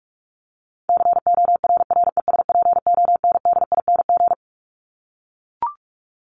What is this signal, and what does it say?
Morse code, heard as a keyed tone, about 44 wpm, 700 Hz. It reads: CORRESPONDING
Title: ING Suffix - Top 100 Words with Base Words 45wpm